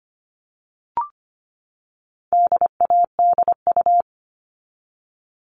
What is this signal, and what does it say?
Morse code, heard as a keyed tone, about 25 wpm, 700 Hz. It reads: DADU
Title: All Cut Numbers - 4-Digit 25wpm